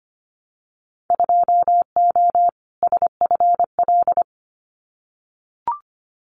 Morse code, heard as a keyed tone, about 25 wpm, 700 Hz. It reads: 2O SFL
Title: Winter Field Day Exchanges 25wpm